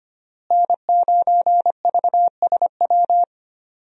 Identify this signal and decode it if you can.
Morse code, heard as a keyed tone, about 25 wpm, 700 Hz. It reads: N9VSW
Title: USA Call Signs - 5 Characters - Repeated 2x - 25wpm